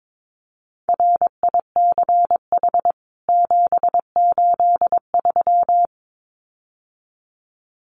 Morse code, heard as a keyed tone, about 22 wpm, 700 Hz. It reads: RICH 783
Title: CWT Exchange 22wpm